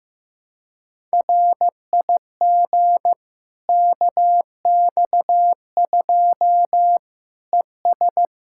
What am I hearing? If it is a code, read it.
Morse code, heard as a keyed tone, about 16 wpm, 700 Hz. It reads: RIG KX2 ES
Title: K1USN SST QSO Practice 15wpm